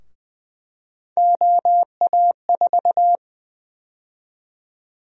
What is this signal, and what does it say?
Morse code, heard as a keyed tone, about 20 wpm, 700 Hz. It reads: OA4
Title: USA Call Sign Prefixes - 20wpm